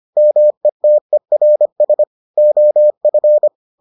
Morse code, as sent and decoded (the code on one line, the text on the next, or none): -- . - . .-. ... / --- ..-.
METERS OF